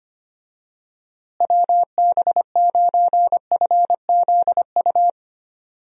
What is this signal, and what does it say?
Morse code, heard as a keyed tone, about 25 wpm, 700 Hz. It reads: WB9FZU
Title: USA Call Signs - 6 Characters - Repeated 2x - 25wpm